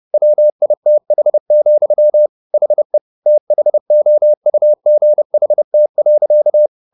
Morse code, heard as a keyed tone, about 30 wpm, 600 Hz. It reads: WITH, HE THOUGHT.